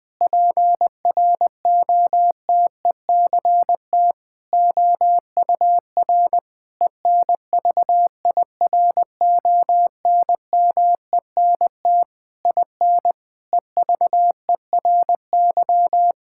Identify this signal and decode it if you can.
Morse code, heard as a keyed tone, about 20 wpm, 700 Hz. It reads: PROTECT OUR ENVIRONMENT IN EVERY
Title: Sentences from Top 1000 Words - Review Part 5 - 20wpm